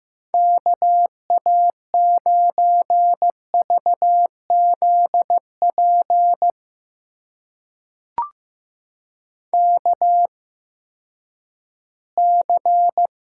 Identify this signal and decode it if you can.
Morse code, heard as a keyed tone, about 15 wpm, spaced as for 10 wpm, 700 Hz. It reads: KA9VZP KC
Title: USA Call Signs - 6 Characters - 8x Character Spacing - 15wpm